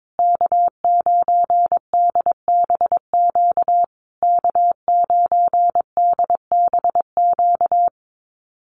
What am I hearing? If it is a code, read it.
Morse code, heard as a keyed tone, about 22 wpm, 700 Hz. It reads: K9DBQ K9DBQ
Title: USA Call Signs Repeated 3x 22wpm